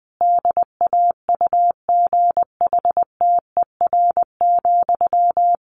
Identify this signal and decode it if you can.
Morse code, heard as a keyed tone, about 20 wpm, 700 Hz. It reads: DAUGHTER,
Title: Sentences from Top 1500 Words 20wpm